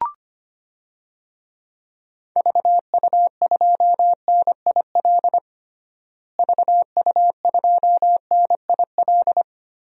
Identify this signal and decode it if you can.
Morse code, heard as a keyed tone, about 25 wpm, 700 Hz. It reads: VU2NIL VU2NIL
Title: DXCC-100 Call Signs - 6 Characters - Repeated 2x - 25wpm